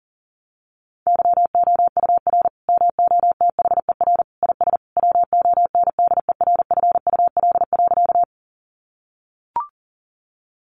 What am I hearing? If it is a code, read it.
Morse code, heard as a keyed tone, about 37 wpm, 700 Hz. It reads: YOUR MOTHER IS WONDERFUL.
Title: Sentences from Top 1100 Words 40wpm